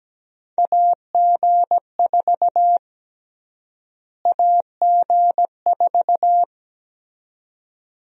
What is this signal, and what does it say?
Morse code, heard as a keyed tone, about 17 wpm, 700 Hz. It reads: AG4 AG4
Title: USA Call Sign Prefixes - Repeated 2x - 17wpm